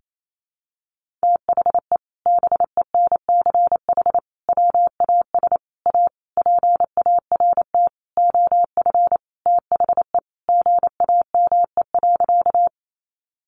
Morse code, heard as a keyed tone, about 28 wpm, 700 Hz. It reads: THE BENCH WAS A PART OF THE GAME.
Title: Sentences from Top 1400 Words 28wpm